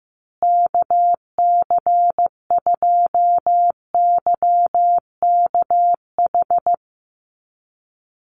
Morse code, heard as a keyed tone, about 15 wpm, 700 Hz. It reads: KC2YKH
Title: USA Call Signs Repeated 2x 15wpm